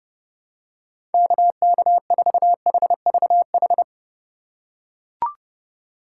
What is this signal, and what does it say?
Morse code, heard as a keyed tone, about 30 wpm, 700 Hz. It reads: KK4HVH
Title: USA Call Signs - 6 Characters - 2x Character Spacing - 30wpm